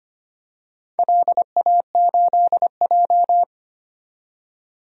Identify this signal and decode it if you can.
Morse code, heard as a keyed tone, about 25 wpm, 700 Hz. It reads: LA8J
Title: DXCC-100 Call Signs - 4 Characters - 25wpm